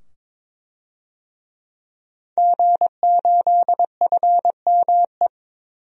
Morse code, heard as a keyed tone, about 21 wpm, 700 Hz. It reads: G8FME
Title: DXCC-100 Call Signs - 5 Characters - 22wpm